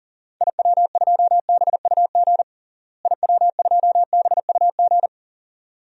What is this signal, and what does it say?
Morse code, heard as a keyed tone, about 40 wpm, 700 Hz. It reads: IW2BUG IW2BUG
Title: DXCC-100 Call Signs - 6 Characters - Repeated 2x - 40wpm